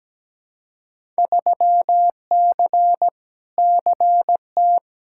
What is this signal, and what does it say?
Morse code, heard as a keyed tone, about 17 wpm, 700 Hz. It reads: 3C CT
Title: Field Day Exchanges Encore - 17wpm